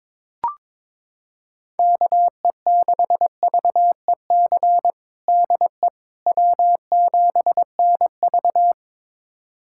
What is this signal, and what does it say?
Morse code, heard as a keyed tone, about 22 wpm, 700 Hz. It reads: KE6VEC DE W7NV
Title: USA Callsign DE Callsign 22wpm